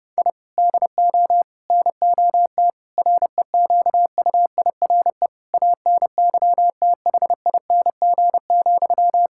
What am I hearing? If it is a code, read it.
Morse code, heard as a keyed tone, about 32 wpm, 700 Hz. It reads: I DO NOT REQUIRE ANYTHING,
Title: Sentences from Top 1000 Words - Review Part 6 - 30wpm